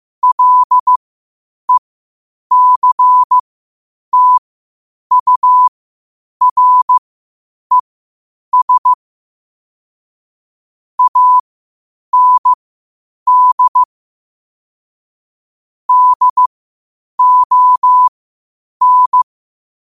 Morse code, heard as a keyed tone, about 15 wpm, spaced as for 8 wpm, 1000 Hz. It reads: LECTURES AND DON
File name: qotdfiles / 8wpm